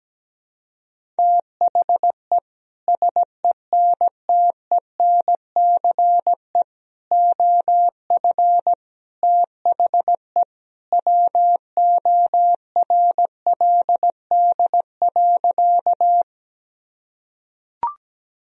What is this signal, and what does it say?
Morse code, heard as a keyed tone, about 17 wpm, 700 Hz. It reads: THE SENTENCE OF THE WORLD.
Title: Sentences from Top 1000 Words - Review Part 1 - 17wpm